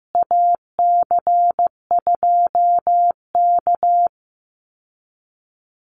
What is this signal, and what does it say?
Morse code, heard as a keyed tone, about 15 wpm, 700 Hz. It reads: AC2K
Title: Call Signs Repeated 2x - 15wpm